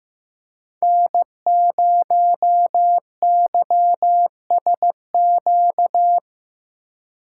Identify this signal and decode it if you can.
Morse code, heard as a keyed tone, about 15 wpm, 700 Hz. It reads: N0YSQ
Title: K1USN SST QSO Practice - 15wpm